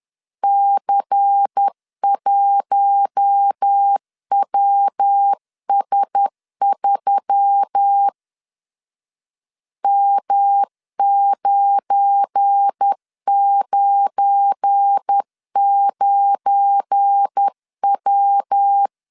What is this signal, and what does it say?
Morse code, heard as a keyed tone, about 10 wpm, 800 Hz. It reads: C1WS3 M999W
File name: LEZIONI / 7 GF9Z / 7 GF9 10wpm 2747